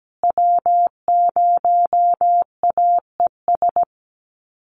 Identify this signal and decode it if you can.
Morse code, heard as a keyed tone, about 17 wpm, 700 Hz. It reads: W0AES
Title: USA Call Signs Repeated 3x 17wpm